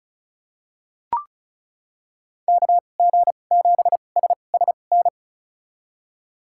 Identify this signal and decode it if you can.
Morse code, heard as a keyed tone, about 34 wpm, spaced as for 25 wpm, 700 Hz. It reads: KG7SSN
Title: USA Call Signs - 6 Characters - 2x Character Spacing - 35wpm